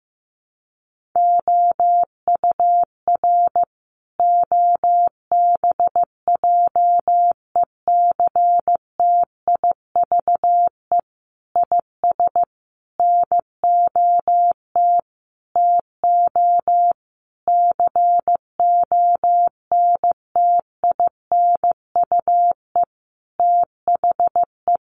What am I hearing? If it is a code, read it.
Morse code, heard as a keyed tone, about 15 wpm, 700 Hz. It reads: OUR OBJECTIVE IS NOT TO CONTINUE THE
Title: Sentences from Top 1200 Words 15wpm